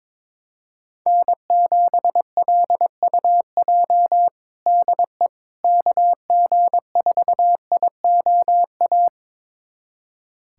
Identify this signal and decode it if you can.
Morse code, heard as a keyed tone, about 22 wpm, 700 Hz. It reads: N7LUJ DE KG4IOA
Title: USA Callsign DE Callsign 22wpm